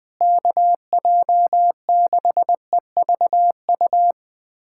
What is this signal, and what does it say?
Morse code, heard as a keyed tone, about 21 wpm, 700 Hz. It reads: KJ6EVU
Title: USA Callsign DE Callsign - 6 Characters - 20wpm